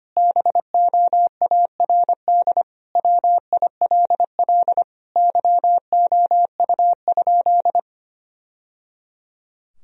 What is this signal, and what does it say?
Morse code, heard as a keyed tone, about 25 wpm, 700 Hz. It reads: BOARD WILL YOU?
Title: Sentences from Top 1000 Words - Review Part 4 - 25wpm